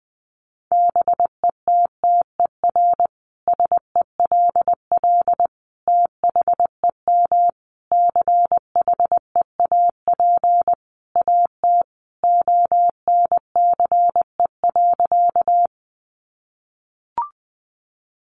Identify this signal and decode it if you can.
Morse code, heard as a keyed tone, about 19 wpm, 700 Hz. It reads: BETTER SELL THEM CHEAP AT ONCE.
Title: Sentences from Top 1200 Words 20wpm